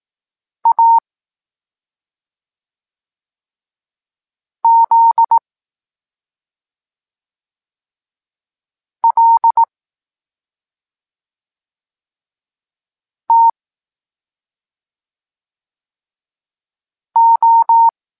Morse code, heard as a keyed tone, about 18 wpm, 920 Hz. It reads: A Z L T O
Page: Tekst 23.3 – Ciąg dalszy doskonalenia odbioru poznanych liter